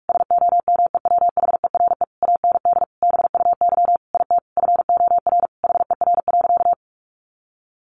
Morse code, heard as a keyed tone, about 44 wpm, 700 Hz. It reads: SOMEWHERE AND BUY IT FOR HER.
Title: Sentences from Top 1400 Words 45wpm